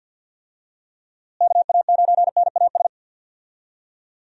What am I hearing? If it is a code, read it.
Morse code, heard as a keyed tone, about 49 wpm, 700 Hz. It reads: KA9NAS
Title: USA Call Signs - 6 Characters - Repeated 2x - 50wpm